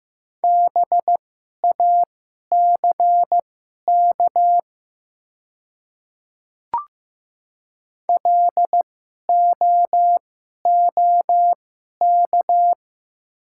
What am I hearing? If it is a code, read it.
Morse code, heard as a keyed tone, about 15 wpm, spaced as for 11 wpm, 700 Hz. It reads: BACK LOOK
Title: 2x Character Spacing - Top 500 Words - 15wpm